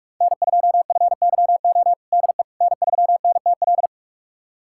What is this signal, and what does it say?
Morse code, heard as a keyed tone, about 42 wpm, 700 Hz. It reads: N2FYO DE N3NTL
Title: USA Callsign DE Callsign - 5 Characters - 45wpm